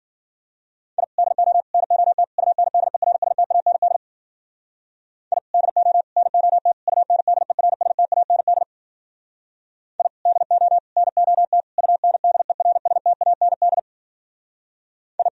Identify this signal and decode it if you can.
Morse code, heard as a keyed tone, about 51 wpm, 700 Hz. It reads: I DO NOT UNDERSTAND I DO NOT UNDERSTAND I DO NOT UNDERSTAND I